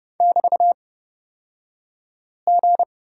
Morse code, heard as a keyed tone, about 30 wpm, spaced as for 5 wpm, 700 Hz. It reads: <BT>G
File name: Lesson 36d - BT - Two Characters 30wpm